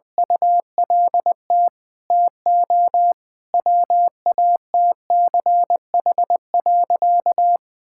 Morse code, heard as a keyed tone, about 21 wpm, 700 Hz. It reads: ULT TO WATCH.
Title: Sentences from Top 1000 Words - Review Part 5 - 20wpm